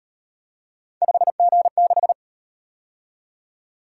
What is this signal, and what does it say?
Morse code, heard as a keyed tone, about 37 wpm, 700 Hz. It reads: 5G6